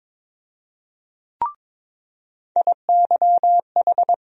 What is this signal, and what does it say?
Morse code, heard as a keyed tone, about 22 wpm, 700 Hz. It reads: IYH
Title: USA Call Sign Postfixes - 3 Letters 22wpm